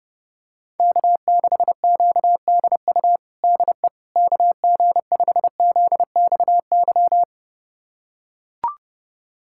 Morse code, heard as a keyed tone, about 30 wpm, 700 Hz. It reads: K6QDU DE KG5ZXY
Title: USA Callsign DE Callsign 30wpm